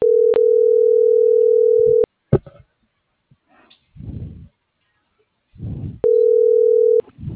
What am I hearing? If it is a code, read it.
Ringback tone (North America), 440 + 480 Hz.